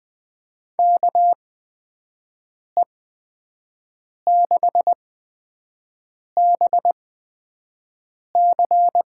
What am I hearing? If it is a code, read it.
Morse code, heard as a keyed tone, about 21 wpm, spaced as for 5 wpm, 700 Hz. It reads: KE6BC
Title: USA Call Signs - 6 Characters - 8x Character Spacing - 20wpm